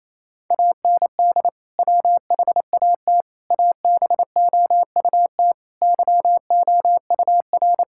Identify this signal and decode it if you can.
Morse code, heard as a keyed tone, about 28 wpm, 700 Hz. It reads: AND WHAT ABOUT YOUR
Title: Sentences from Top 1000 Words - Review Part 6 - 28wpm